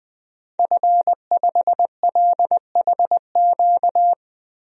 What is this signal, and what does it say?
Morse code, heard as a keyed tone, about 21 wpm, 700 Hz. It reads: F5LHQ
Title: DXCC-100 Call Signs - 5 Characters - Repeated 3x - 20wpm